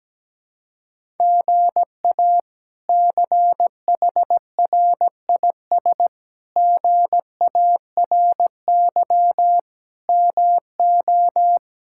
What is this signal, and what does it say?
Morse code, heard as a keyed tone, about 17 wpm, 700 Hz. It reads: GA CHRIS GARY MO
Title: K1USN SST QSO Practice - 17wpm